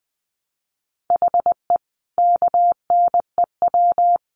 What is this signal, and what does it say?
Morse code, heard as a keyed tone, about 20 wpm, 700 Hz. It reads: HE KNEW
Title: Sentences from Top 500 Words 20wpm